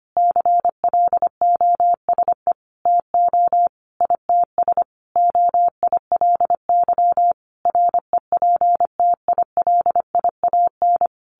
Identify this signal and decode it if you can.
Morse code, heard as a keyed tone, about 25 wpm, 700 Hz. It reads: CLOSE TO ITS OILY REPTILIAN
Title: A Sound Of Thunder 25wpm